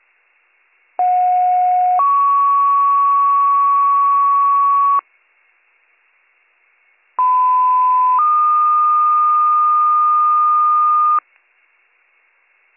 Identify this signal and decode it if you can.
Two-tone page, heard as a one-second tone, then a three-second tone, about 730 then 1120 Hz, about 1010 then 1220 Hz. It is heard from 1.0 to 11.2 s.